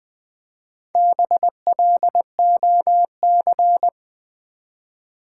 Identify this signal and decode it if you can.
Morse code, heard as a keyed tone, about 20 wpm, 700 Hz. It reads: BLOC
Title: Build A Word - 1 Letter At A Time - 5-Letter Words - 20wpm